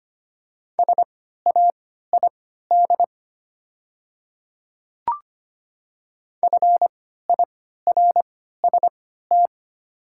Morse code, heard as a keyed tone, about 25 wpm, spaced as for 14 wpm, 700 Hz. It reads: SAID FIRST
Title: 3x Character Spacing - Top 100 Words - 25wpm